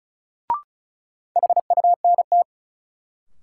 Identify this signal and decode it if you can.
Morse code, heard as a keyed tone, about 36 wpm, 700 Hz. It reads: HUNT
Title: Sentences from Top 1600 Words - New Word Drill - 35wpm